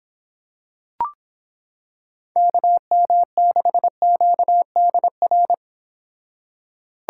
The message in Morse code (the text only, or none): KM6QDR